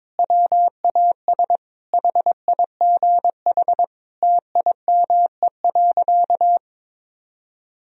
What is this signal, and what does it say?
Morse code, heard as a keyed tone, about 21 wpm, 700 Hz. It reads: WAS HIGH TIME.
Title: Sentences from Top 1000 Words - Review Part 1 - 22wpm